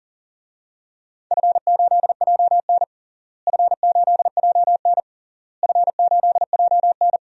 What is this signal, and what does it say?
Morse code, heard as a keyed tone, about 40 wpm, 700 Hz. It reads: F8JN F8JN F8JN
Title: DXCC-100 Call Signs - 4 Characters - Repeated 3x - 40wpm